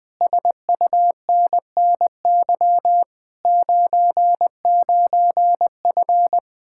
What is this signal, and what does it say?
Morse code, heard as a keyed tone, about 20 wpm, 700 Hz. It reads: SUNNY 99F
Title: Speed-Racing Common QSO Phrases - 20wpm